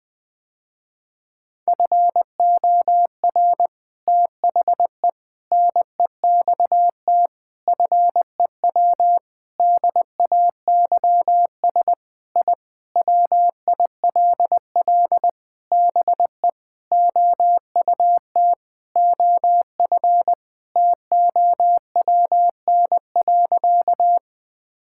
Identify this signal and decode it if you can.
Morse code, heard as a keyed tone, about 21 wpm, 700 Hz. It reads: FOR THE NEXT FEW DAYS I WILL BE OUT OF TOWN.
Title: Sentences with 3-Word Phrases - Spoken Beginning - 20wpm